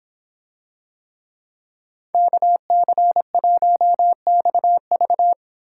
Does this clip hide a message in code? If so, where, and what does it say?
Yes: Morse code from 2.1 to 5.3 s, heard as a keyed tone, about 26 wpm, 700 Hz. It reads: KC1XV